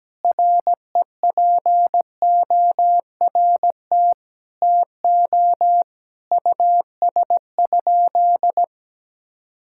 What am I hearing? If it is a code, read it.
Morse code, heard as a keyed tone, about 17 wpm, 700 Hz. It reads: REPORT TO US?
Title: Sentences from Top 1000 Words - Review Part 6 - 17wpm